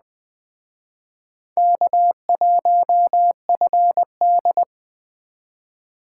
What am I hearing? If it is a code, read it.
Morse code, heard as a keyed tone, about 20 wpm, 700 Hz. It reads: K1FD